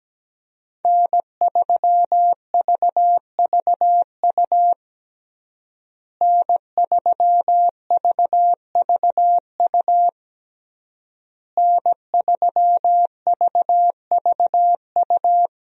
Morse code, heard as a keyed tone, about 17 wpm, 700 Hz. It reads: N3VVU N3VVU N3VVU
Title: USA Call Signs - 5 Characters - Repeated 3x - 17wpm